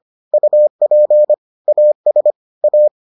Morse code, heard as a keyed tone, about 25 wpm, 600 Hz. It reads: UP AS A